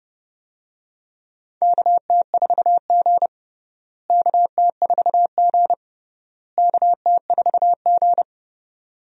Morse code, heard as a keyed tone, about 32 wpm, 700 Hz. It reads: KT4G KT4G KT4G
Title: USA Call Signs - 4 Characters - Repeated 3x - 30wpm